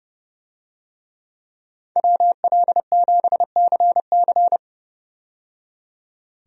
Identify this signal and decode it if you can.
Morse code, heard as a keyed tone, about 31 wpm, 700 Hz. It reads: WL7CC
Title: USA Call Signs - 5 Characters - 30wpm